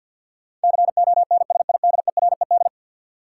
Morse code, heard as a keyed tone, about 47 wpm, 700 Hz. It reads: CONSIDERED
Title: Sets of 5 Words - New Word Drill - 50wpm